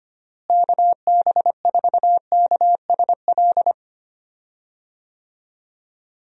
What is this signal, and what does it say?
Morse code, heard as a keyed tone, about 25 wpm, 700 Hz. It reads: KB4KSL
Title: USA Call Signs - 6 Characters - Repeated 3x - 25wpm